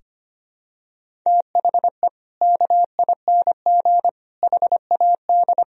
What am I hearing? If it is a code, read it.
Morse code, heard as a keyed tone, about 25 wpm, 700 Hz. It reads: THE KING HAD
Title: Sentences from Top 1000 Words - Review Part 2 - 25wpm